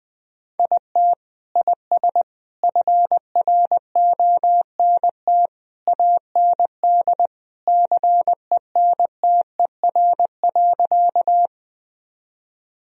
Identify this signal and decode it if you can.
Morse code, heard as a keyed tone, about 21 wpm, 700 Hz. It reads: IT IS FRONT AND CENTER.
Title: Sentences from Top 1000 Words - Review Part 2 - 20wpm